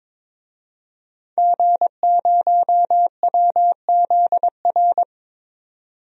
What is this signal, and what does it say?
Morse code, heard as a keyed tone, about 22 wpm, 700 Hz. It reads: G0WZR
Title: DXCC-100 Call Signs - 5 Characters - 22wpm